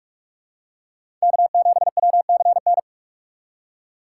Morse code, heard as a keyed tone, about 44 wpm, 700 Hz. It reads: K7WCN